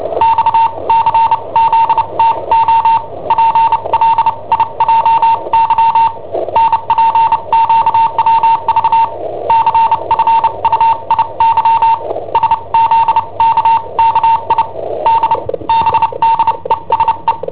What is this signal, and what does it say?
Morse code, heard as a keyed tone, about 29 wpm, spaced as for 24 wpm, 910 Hz. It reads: XCZTO PLIJY NPQWV CFUIY SZKKID BDESE
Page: Cassetta esercizi preparata da I7OHP per IW5ELP